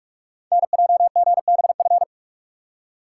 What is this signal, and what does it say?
Morse code, heard as a keyed tone, about 44 wpm, 690 Hz. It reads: NJGBF
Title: Code Groups - 5 Letters - 45wpm